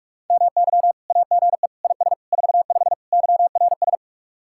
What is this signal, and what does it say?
Morse code, heard as a keyed tone, about 48 wpm, 700 Hz. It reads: MY AGE IS 45 YRS